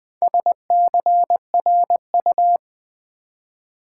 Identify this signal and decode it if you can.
Morse code, heard as a keyed tone, about 20 wpm, 700 Hz. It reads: SCRU
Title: Build A Word - 2 Letters At A Time - 8-Letter Words - 20wpm